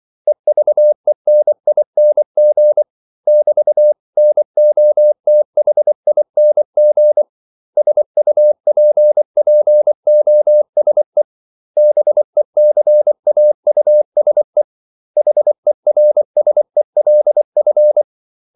Morse code, heard as a keyed tone, about 24 wpm, 600 Hz. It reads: EVENING <BT>NOTHING SUPPOSE BECAUSE HERSELF
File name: CW / Practice / 500MostCommonEnglishWords24